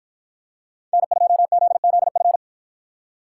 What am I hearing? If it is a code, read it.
Morse code, heard as a keyed tone, about 53 wpm, 700 Hz. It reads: N2ZZF